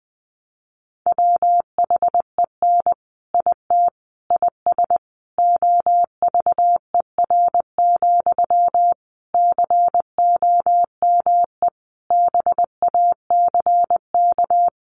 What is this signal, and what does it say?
Morse code, heard as a keyed tone, about 20 wpm, 700 Hz. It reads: WHEN IT IS OVER, COME BACK
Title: Sentences from Top 1200 Words 20wpm